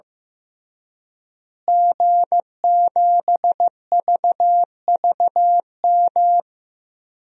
Morse code, heard as a keyed tone, about 15 wpm, 700 Hz. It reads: G7VVM
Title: DXCC-100 Call Signs - 5 Characters - Repeated 3x - 15wpm